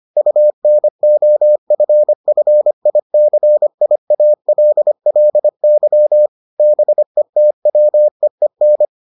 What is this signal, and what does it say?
Morse code, heard as a keyed tone, about 25 wpm, 600 Hz. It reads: UNOFFICIALLY BETWEEN